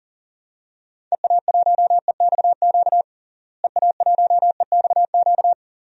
Morse code, heard as a keyed tone, about 40 wpm, 700 Hz. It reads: EA1EXQ EA1EXQ
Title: DXCC-100 Call Signs - 6 Characters - Repeated 2x - 40wpm